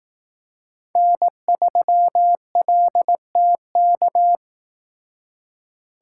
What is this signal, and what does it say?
Morse code, heard as a keyed tone, about 18 wpm, 700 Hz. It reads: N3LTK
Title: Speed Racing - USA 1x3 Call Signs 15wpm